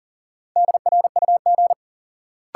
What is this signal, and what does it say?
Morse code, heard as a keyed tone, about 39 wpm, 700 Hz. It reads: DRUG
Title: Sets of 5 Words - New Word Drill - 40wpm